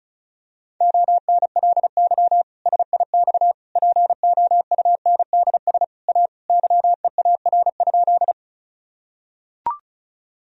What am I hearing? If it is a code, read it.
Morse code, heard as a keyed tone, about 36 wpm, 700 Hz. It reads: ONLY SIX POUNDS A YEAR?
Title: Sentences from Top 1000 Words - Review Part 2 - 35wpm